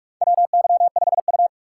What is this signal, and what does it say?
Morse code, heard as a keyed tone, about 47 wpm, 700 Hz. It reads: WY5U
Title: USA Call Signs - 3x Character Spacing - 45wpm